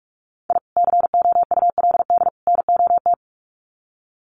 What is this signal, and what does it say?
Morse code, heard as a keyed tone, about 48 wpm, 700 Hz. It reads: I COULD NOT
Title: Sets of 3 Words 45wpm